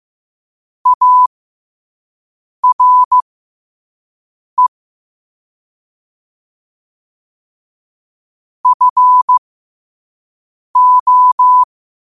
Morse code, heard as a keyed tone, about 15 wpm, spaced as for 5 wpm, 1000 Hz. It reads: ARE FO
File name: qotdfiles / 5wpm